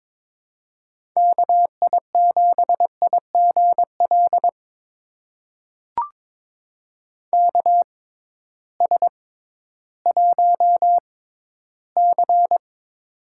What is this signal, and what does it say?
Morse code, heard as a keyed tone, about 22 wpm, spaced as for 13 wpm, 700 Hz. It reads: KI7IGL KS1C
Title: USA Call Signs - 6x Character Spacing - 22wpm